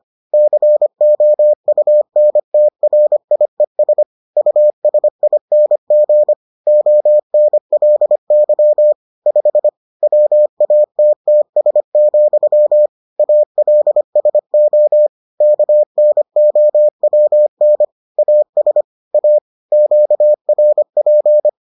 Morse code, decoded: COUNTRIES USING ONLY 5 WATTS, ALSO KNOWN AS A QRP

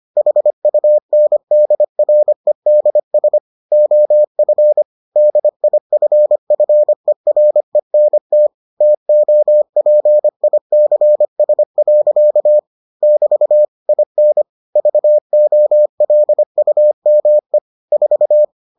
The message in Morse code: HUNDREDS OF DIFFERENT TOPICS. <BT> IN VOLUME 4

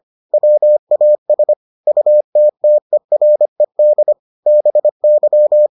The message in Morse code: WAS UTTERED BY

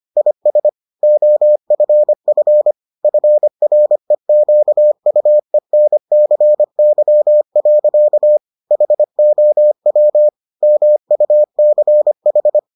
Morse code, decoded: IS OFF FREQUENCY. HOW MUCH

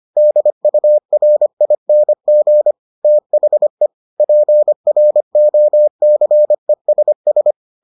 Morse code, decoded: DURING THE PROCESS

